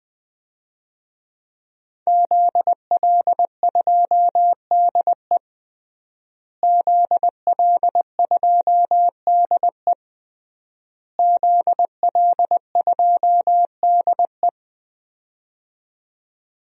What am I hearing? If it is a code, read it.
Morse code, heard as a keyed tone, about 20 wpm, 700 Hz. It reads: ZL2DE ZL2DE ZL2DE
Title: DXCC-100 Call Signs - 5 Characters - Repeated 3x - 20wpm